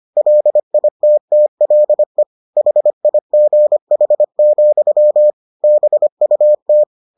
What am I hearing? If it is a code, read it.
Morse code, heard as a keyed tone, about 25 wpm, 600 Hz. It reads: LITTLE HIGH, BUT